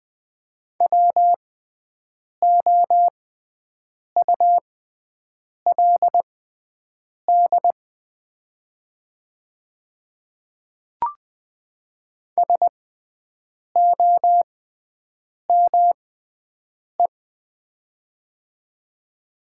Morse code, heard as a keyed tone, about 20 wpm, spaced as for 7 wpm, 700 Hz. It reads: WOULD SOME
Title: 6x Character Spacing - Top 100 Words - 20wpm